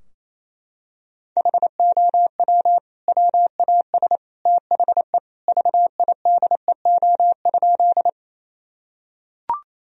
Morse code, heard as a keyed tone, about 28 wpm, 700 Hz. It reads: HOW WAS THE VIDEO?
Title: Sentences from Top 1000 Words - Review Part 4 - 28wpm